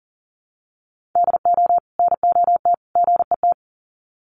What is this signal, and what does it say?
Morse code, heard as a keyed tone, about 40 wpm, 700 Hz. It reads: DO NOT GET
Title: Sets of 3 Words 40wpm